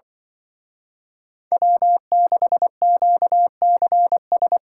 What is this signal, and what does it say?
Morse code, heard as a keyed tone, about 24 wpm, 700 Hz. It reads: W6QCS
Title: Speed Racing - USA 1x3 Call Signs 20wpm